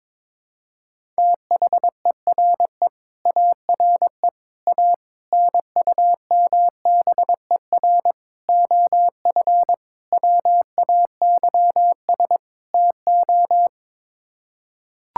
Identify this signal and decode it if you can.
Morse code, heard as a keyed tone, about 22 wpm, 700 Hz. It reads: THERE ARE A NUMBER OF WAYS TO
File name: practice / 22wpm / Sets of 7 Words - 22wpm